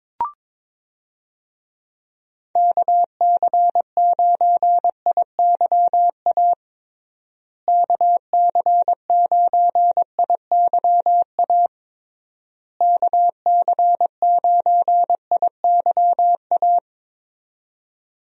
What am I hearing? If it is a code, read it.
Morse code, heard as a keyed tone, about 22 wpm, 700 Hz. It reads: KC9IYA KC9IYA KC9IYA
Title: USA Call Signs - 6 Characters - Repeated 3x - 22wpm